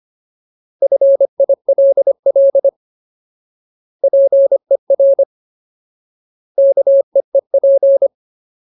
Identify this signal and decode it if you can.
Morse code, heard as a keyed tone, about 25 wpm, 550 Hz. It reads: FILL PER KEEP
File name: Ultimate Word List / 25WPM Ultimate